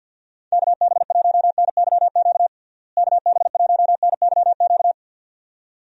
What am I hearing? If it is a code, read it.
Morse code, heard as a keyed tone, about 51 wpm, 700 Hz. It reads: KB1NYQ KB1NYQ
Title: USA Call Signs - 6 Characters - Repeated 2x - 50wpm